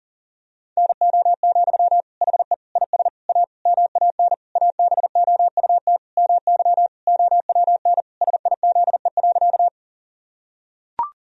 Morse code, heard as a keyed tone, about 40 wpm, 700 Hz. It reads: NO, HE IS A MAN ABOUT MY OWN SIZE.
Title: Sentences from Top 1000 Words - Review Part 2 - 40wpm